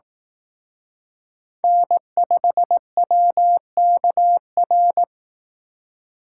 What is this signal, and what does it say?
Morse code, heard as a keyed tone, about 18 wpm, 700 Hz. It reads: N5WKR
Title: Speed Racing - USA 1x3 Call Signs 15wpm